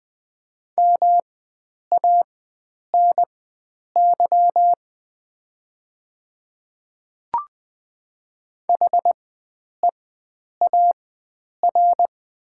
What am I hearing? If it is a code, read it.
Morse code, heard as a keyed tone, about 20 wpm, spaced as for 9 wpm, 700 Hz. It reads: MANY HEAR
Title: 4x Character Spacing - Top 300 Words - 20wpm